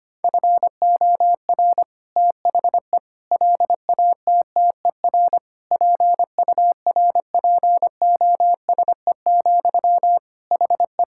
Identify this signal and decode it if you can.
Morse code, heard as a keyed tone, about 25 wpm, 700 Hz. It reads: FOR THE LATTER PURPOSE, HE
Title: Sentences from Top 1600 Words 25wpm